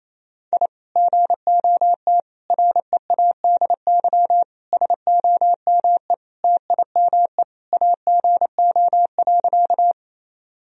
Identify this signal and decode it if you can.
Morse code, heard as a keyed tone, about 28 wpm, 700 Hz. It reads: I GOT READY SOME TIME AGO.
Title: Sentences from Top 1000 Words - Review Part 3 - 28wpm